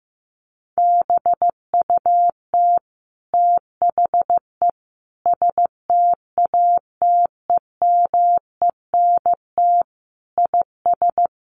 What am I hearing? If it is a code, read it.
Morse code, heard as a keyed tone, about 15 wpm, 700 Hz. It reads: BUT THE STATEMENT IS
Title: Sentences from Top 1200 Words 15wpm